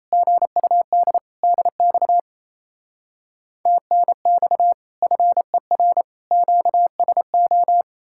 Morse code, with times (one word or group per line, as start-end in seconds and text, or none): 0.1-1.2 GUD
1.4-2.2 DX
3.7-4.7 TNX
5.0-6.0 FER
6.3-7.8 QSO